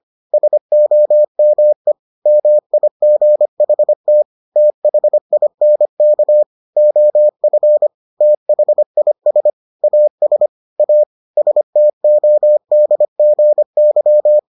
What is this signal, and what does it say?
Morse code, heard as a keyed tone, about 25 wpm, 600 Hz. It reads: SOME MIGHT THINK OF THIS AS A STODGY